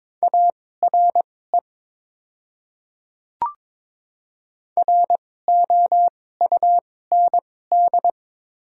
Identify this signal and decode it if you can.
Morse code, heard as a keyed tone, about 22 wpm, spaced as for 16 wpm, 700 Hz. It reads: ARE ROUND
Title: 2x Character Spacing - Top 500 Words - 22wpm